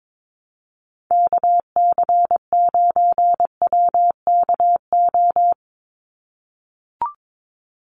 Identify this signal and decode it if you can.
Morse code, heard as a keyed tone, about 22 wpm, 700 Hz. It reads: KC9WKO
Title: USA Call Signs 22wpm